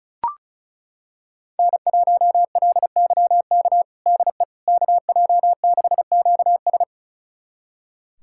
Morse code, heard as a keyed tone, about 36 wpm, 700 Hz. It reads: N1LYK DE KJ6QS
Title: USA Callsign DE Callsign - 5 Characters - 35wpm